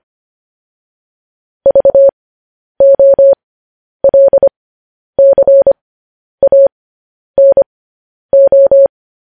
Morse code, heard as a keyed tone, about 25 wpm, spaced as for 10 wpm, 550 Hz. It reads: VOLCANO
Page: news headlines in morse code at 10 wpm